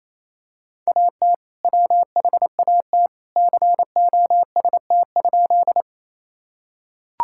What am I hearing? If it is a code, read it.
Morse code, heard as a keyed tone, about 28 wpm, 700 Hz. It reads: AT WHAT COST?
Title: Sentences from Top 1000 Words - Review Part 5 - 28wpm